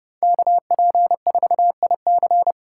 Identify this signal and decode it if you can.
Morse code, heard as a keyed tone, about 30 wpm, 700 Hz. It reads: KP4IC